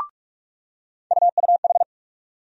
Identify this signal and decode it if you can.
Morse code, heard as a keyed tone, about 43 wpm, 700 Hz. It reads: UUH
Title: USA Call Sign Postfixes - 3 Letters 45wpm